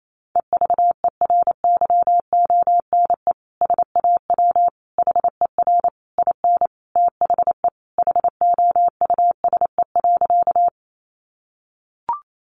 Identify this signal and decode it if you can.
Morse code, heard as a keyed tone, about 28 wpm, 700 Hz. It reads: EVERYONE SAW HER IN THE HOUSE.
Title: Sentences from Top 800 Words 28wpm